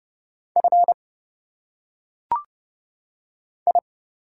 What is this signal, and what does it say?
Morse code, heard as a keyed tone, about 30 wpm, 700 Hz. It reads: F I
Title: Single Letters - Phonetic Pronunciation 30wpm